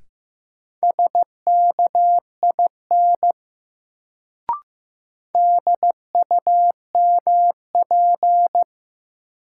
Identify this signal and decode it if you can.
Morse code, heard as a keyed tone, about 15 wpm, 700 Hz. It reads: SKIN DUMP
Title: Sentences from Top 1300 Words - New Word Drill Alt - 15wpm